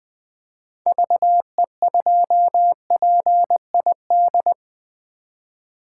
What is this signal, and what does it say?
Morse code, heard as a keyed tone, about 20 wpm, 700 Hz. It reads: VE2PID
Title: 2023 SOTA Chaser Call Signs - 20wpm